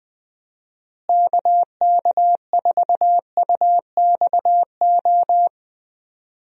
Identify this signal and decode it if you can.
Morse code, heard as a keyed tone, about 20 wpm, 700 Hz. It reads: KK4UXO
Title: USA Call Signs - 6 Characters - Repeated 3x - 20wpm